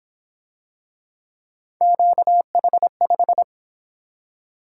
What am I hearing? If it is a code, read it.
Morse code, heard as a keyed tone, about 26 wpm, 700 Hz. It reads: QH5
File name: Speed-Racing - Three Letter-Number Combinations - 22wpm